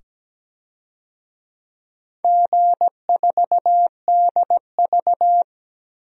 Morse code, heard as a keyed tone, about 17 wpm, 700 Hz. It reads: G4DV
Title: DXCC-100 Call Signs - 4 Characters - 17wpm